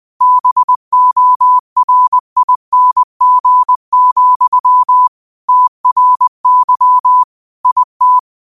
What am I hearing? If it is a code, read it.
Morse code, heard as a keyed tone, about 20 wpm, 1000 Hz. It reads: BORING, TRY IT
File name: qotdfiles / 20wpm